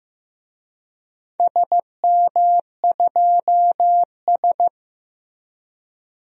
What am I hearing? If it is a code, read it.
Morse code, heard as a keyed tone, about 15 wpm, 700 Hz. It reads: SM2S
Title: DXCC-100 Call Signs - 4 Characters - Repeated 3x - 15wpm